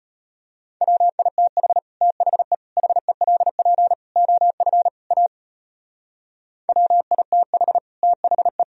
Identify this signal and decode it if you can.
Morse code, heard as a keyed tone, about 31 wpm, 700 Hz. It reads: WITH THE HELP OF A WITH THE
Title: Speed-Racing - Sets of 5 Words 25wpm